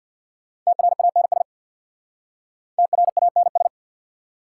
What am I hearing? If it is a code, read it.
Morse code, heard as a keyed tone, about 59 wpm, 690 Hz. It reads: TRANS TRANS
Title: Speed-Racing Prefixes and Suffixes 50wpm